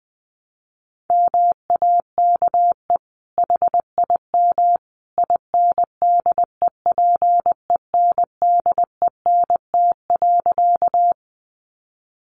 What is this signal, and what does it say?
Morse code, heard as a keyed tone, about 20 wpm, 700 Hz. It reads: MAKE HIM INDEPENDENT.
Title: Sentences from Top 900 Words 20wpm